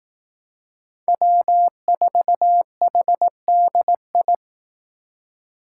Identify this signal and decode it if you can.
Morse code, heard as a keyed tone, about 18 wpm, 700 Hz. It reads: W4HDI
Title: Speed Racing - USA 1x3 Call Signs 15wpm